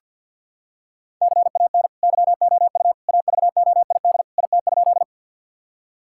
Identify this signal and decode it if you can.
Morse code, heard as a keyed tone, about 50 wpm, 700 Hz. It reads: CAN YOU AVOID IT?
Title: Sentences from Top 1000 Words - Review Part 6 - 50wpm